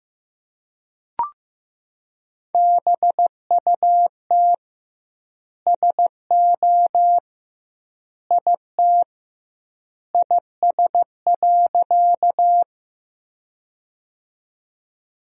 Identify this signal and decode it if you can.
Morse code, heard as a keyed tone, about 15 wpm, 700 Hz. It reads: BUT SO IT IS.
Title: 2x Word-Spacing - Sentences from Top 100 Words - 15wpm